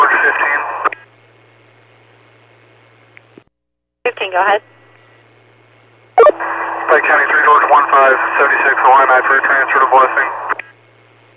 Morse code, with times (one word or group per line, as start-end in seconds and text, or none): none